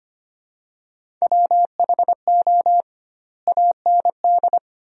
Morse code, heard as a keyed tone, about 25 wpm, 700 Hz. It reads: WHO AND
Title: 2x Word-Spacing - Sentences from Top 100 Words - 25wpm